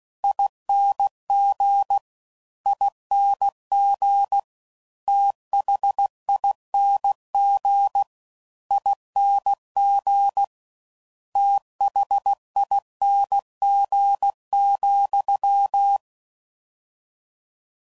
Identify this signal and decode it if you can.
Morse code, heard as a keyed tone, about 16 wpm, 800 Hz. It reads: ING ING THING ING THING,
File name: Suffixes / Special INGs / Slow With Faster Ings